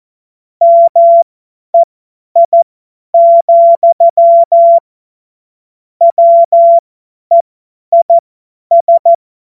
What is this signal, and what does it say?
Morse code, heard as a keyed tone, about 14 wpm, spaced as for 10 wpm, 680 Hz. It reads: MEI, WEIS